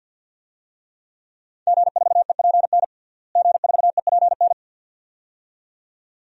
Morse code, heard as a keyed tone, about 51 wpm, 700 Hz. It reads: G4EPN G4EPN
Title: DXCC-100 Call Signs - 5 Characters - Repeated 2x - 50wpm